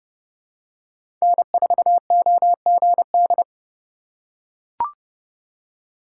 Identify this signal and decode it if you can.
Morse code, heard as a keyed tone, about 30 wpm, 700 Hz. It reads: N4OGD
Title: USA Call Signs - 5 Characters - 3x Character Spacing - 30wpm